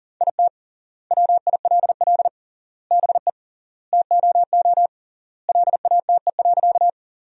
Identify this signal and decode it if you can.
Morse code, heard as a keyed tone, about 40 wpm, 700 Hz. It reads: IT WILL BE TOO LATE.
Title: 3x Word-Spacing - Sentences from Top 300 Words - 40wpm